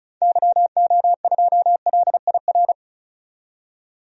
Morse code, heard as a keyed tone, about 34 wpm, 700 Hz. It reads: YO2LIR
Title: DXCC-100 Call Signs - 6 Characters - 35wpm